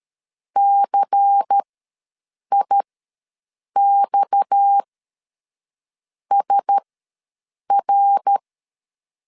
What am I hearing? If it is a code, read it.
Morse code, heard as a keyed tone, about 13 wpm, spaced as for 6 wpm, 800 Hz. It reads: CIXSR
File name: LEZIONI / 8 YJ45 / 8 YJ ws 12wpm 3342